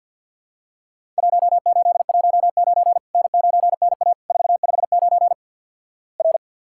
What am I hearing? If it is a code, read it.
Morse code, heard as a keyed tone, about 54 wpm, 700 Hz. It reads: 1819 N9NA 459 R
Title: Worked All Europe DX Contest - QTC Exchange - 50wpm